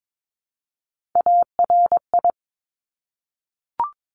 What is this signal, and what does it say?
Morse code, heard as a keyed tone, about 22 wpm, 700 Hz. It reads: ARI
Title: Top Three Letter Combinations in Words 22wpm